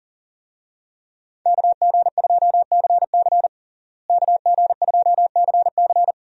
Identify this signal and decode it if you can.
Morse code, heard as a keyed tone, about 40 wpm, 700 Hz. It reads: KG2CC KG2CC
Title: USA Call Signs - 5 Characters - Repeated 2x - 40wpm